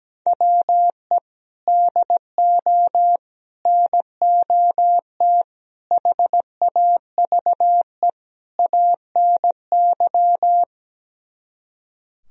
Morse code, heard as a keyed tone, about 17 wpm, 700 Hz. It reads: WE DO NOT HAVE ANY
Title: Speed-Racing - Sets of 5 Words 17wpm